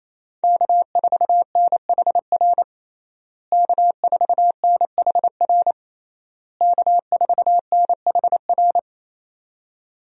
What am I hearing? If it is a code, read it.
Morse code, heard as a keyed tone, about 28 wpm, 700 Hz. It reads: K4NHR K4NHR K4NHR
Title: USA Call Signs - 5 Characters - Repeated 3x - 28wpm